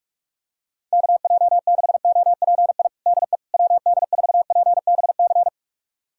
Morse code, heard as a keyed tone, about 41 wpm, 700 Hz. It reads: KJ6OPI DE WD4PBC
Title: USA Callsign DE Callsign - 6 Characters - 45wpm